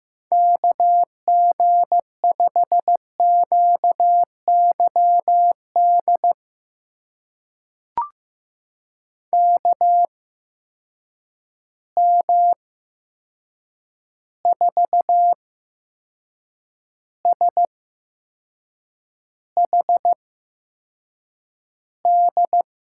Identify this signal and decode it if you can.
Morse code, heard as a keyed tone, about 15 wpm, spaced as for 6 wpm, 700 Hz. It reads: KG5QYD KM4SHD
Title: USA Call Signs - 6 Characters - 8x Character Spacing - 15wpm